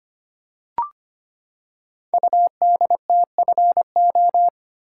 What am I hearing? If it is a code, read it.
Morse code, heard as a keyed tone, about 25 wpm, 700 Hz. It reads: UDTFO
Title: Code Groups - 5 Letters - 25wpm